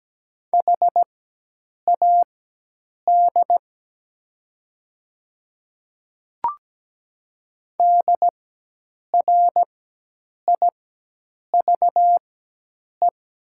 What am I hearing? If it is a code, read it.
Morse code, heard as a keyed tone, about 17 wpm, spaced as for 8 wpm, 700 Hz. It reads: HAD DRIVE
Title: 4x Character Spacing - Top 500 Words - 17wpm